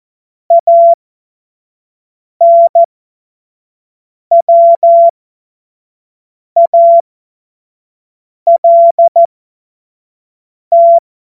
Morse code, heard as a keyed tone, about 14 wpm, spaced as for 5 wpm, 680 Hz. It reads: ANWALT
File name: Morsekurs / Witze / Witz 02. August 2022 25 F70